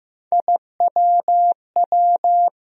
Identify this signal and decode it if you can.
Morse code, heard as a keyed tone, about 15 wpm, 700 Hz. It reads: IWW
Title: USA Call Sign Postfixes - 3 Letters 15wpm